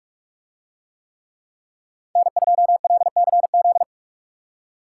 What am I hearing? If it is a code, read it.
Morse code, heard as a keyed tone, about 41 wpm, 700 Hz. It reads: N2LCZ